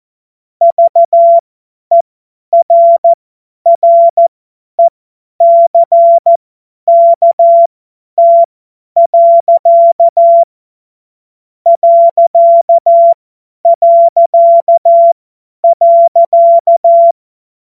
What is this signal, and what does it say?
Morse code, heard as a keyed tone, about 14 wpm, spaced as for 10 wpm, 680 Hz. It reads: VERRECKT. ...
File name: Morsekurs / Witze / Witz 04. April 2023 50 F70